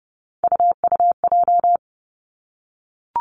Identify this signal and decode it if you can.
Morse code, heard as a keyed tone, about 30 wpm, 700 Hz. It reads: UUJ
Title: Three Challenging Letter-Number Combinations 30wpm